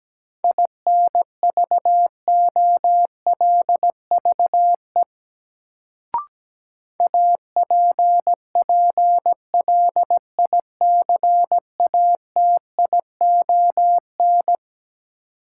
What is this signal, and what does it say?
Morse code, heard as a keyed tone, about 17 wpm, 700 Hz. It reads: INVOLVE APPLICATION
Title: Sentences from Top 1200 Words - New Word Drill - 17wpm